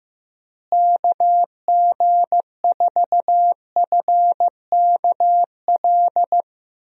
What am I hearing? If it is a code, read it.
Morse code, heard as a keyed tone, about 15 wpm, 700 Hz. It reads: KG4FKL
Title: USA Call Signs - 6 Characters - Repeated 2x - 15wpm